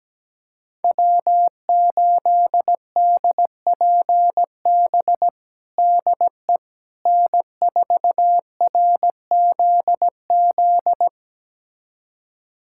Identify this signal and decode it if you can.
Morse code, heard as a keyed tone, about 17 wpm, 700 Hz. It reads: W8DPB DE N4RZZ
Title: USA Callsign DE Callsign - 5 Characters - 17wpm